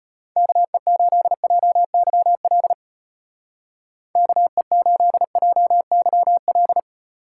Morse code, heard as a keyed tone, about 32 wpm, 700 Hz. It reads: KE8JYL KE8JYL